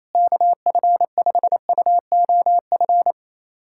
Morse code, heard as a keyed tone, about 28 wpm, 700 Hz. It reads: KF5UOF
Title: USA Callsign DE Callsign - 6 Characters - 28wpm